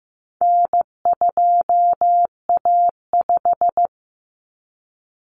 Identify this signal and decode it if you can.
Morse code, heard as a keyed tone, about 15 wpm, 700 Hz. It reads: N2A5
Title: Common Cut Numbers - 4-Digit 15wpm